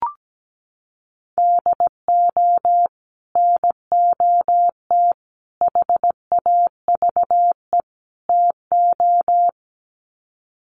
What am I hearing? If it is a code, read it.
Morse code, heard as a keyed tone, about 17 wpm, 700 Hz. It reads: DO NOT HAVE TO
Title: Sets of 4 Words 17wpm